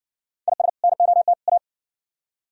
Morse code, heard as a keyed tone, about 59 wpm, 700 Hz. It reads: IS NOT A